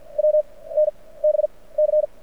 Morse code, heard as a keyed tone, about 54 wpm, spaced as for 21 wpm, 610 Hz. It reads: WNBC
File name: archived / CRSnet